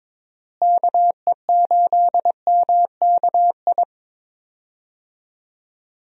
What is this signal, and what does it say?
Morse code, heard as a keyed tone, about 23 wpm, 700 Hz. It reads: KE8MKI
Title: USA Call Signs - 6 Characters - Repeated 3x - 22wpm